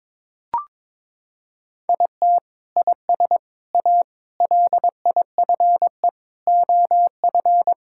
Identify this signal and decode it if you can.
Morse code, heard as a keyed tone, about 22 wpm, 700 Hz. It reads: IT IS A LIFE OF
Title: Sentences from Top 1000 Words - Review Part 5 - 22wpm